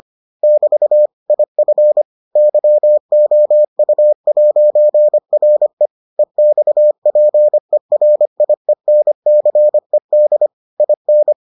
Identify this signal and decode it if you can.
Morse code, heard as a keyed tone, about 25 wpm, 600 Hz. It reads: <BT> IF YOU'RE EXPERIENCED IN